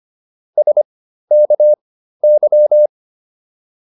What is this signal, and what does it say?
Morse code, heard as a keyed tone, about 25 wpm, spaced as for 13 wpm, 600 Hz. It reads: SKY